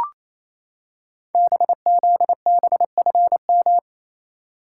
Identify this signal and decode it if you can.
Morse code, heard as a keyed tone, about 29 wpm, 700 Hz. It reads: BZBFM
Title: Code Groups - 5 Letters - 28wpm